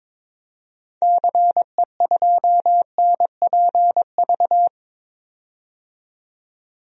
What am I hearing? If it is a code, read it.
Morse code, heard as a keyed tone, about 22 wpm, 700 Hz. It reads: CE2NPV
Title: DXCC-100 Call Signs - 6 Characters - 22wpm